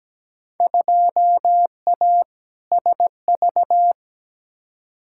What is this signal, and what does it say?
Morse code, heard as a keyed tone, about 17 wpm, 700 Hz. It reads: 2A SV
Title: Field Day Exchanges - 17wpm